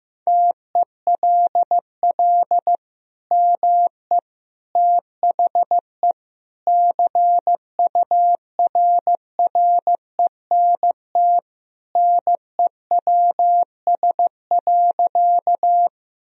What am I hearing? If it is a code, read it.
Morse code, heard as a keyed tone, about 15 wpm, 700 Hz. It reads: TELL ME THE CURRENT NEWS.
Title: Sentences from Top 1000 Words - Review Part 4 - 15wpm